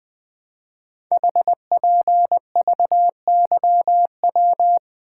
Morse code, heard as a keyed tone, about 20 wpm, 700 Hz. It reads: HPVYW
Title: Code Groups - 5 Letters - 20wpm